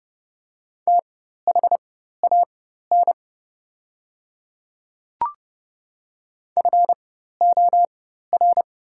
Morse code, heard as a keyed tone, about 30 wpm, spaced as for 14 wpm, 700 Hz. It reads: THAN FOR
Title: 4x Character Spacing - Top 100 Words - 30wpm